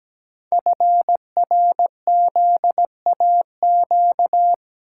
Morse code, heard as a keyed tone, about 17 wpm, 700 Hz. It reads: FRZAQ
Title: Code Groups - 5 Letters - 17wpm